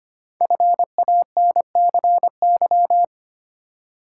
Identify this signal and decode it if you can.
Morse code, heard as a keyed tone, about 25 wpm, 700 Hz. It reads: FANCY
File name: 5 Letter Words - First 3 Letters Given - 25wpm